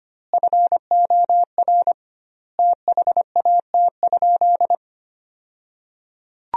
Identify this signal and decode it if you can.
Morse code, heard as a keyed tone, about 25 wpm, 700 Hz. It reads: FOR THAT?
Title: 2x Word-Spacing - Sentences from Top 300 Words - 25wpm